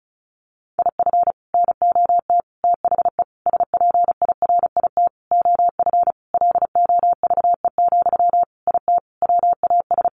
Morse code, heard as a keyed tone, about 34 wpm, 700 Hz. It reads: IF NOT THE SPIRIT OF LOVE, IT WAS
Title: Sentences from Top 1400 Words 35wpm